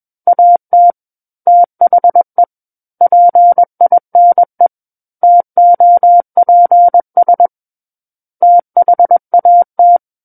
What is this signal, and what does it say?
Morse code, heard as a keyed tone, about 21 wpm, 700 Hz. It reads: AT THE PINE TOPS THAT
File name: Book One / 12-Chapter 12 - 21 WPM